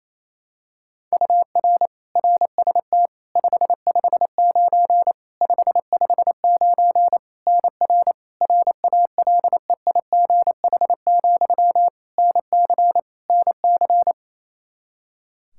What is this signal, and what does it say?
Morse code, heard as a keyed tone, about 28 wpm, 700 Hz. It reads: UR RST 559 559 NR RALEIGH, NC NC
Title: K1USN SST QSO Practice 28wpm